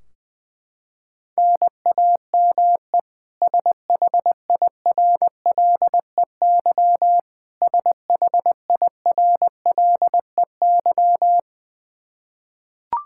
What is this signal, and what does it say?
Morse code, heard as a keyed tone, about 21 wpm, 700 Hz. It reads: NAME SHIRLEY SHIRLEY
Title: K1USN SST QSO Practice 20wpm